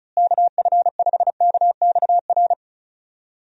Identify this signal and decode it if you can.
Morse code, heard as a keyed tone, about 34 wpm, 700 Hz. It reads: KF5KXR
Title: USA Call Signs - 6 Characters - Repeated 3x - 35wpm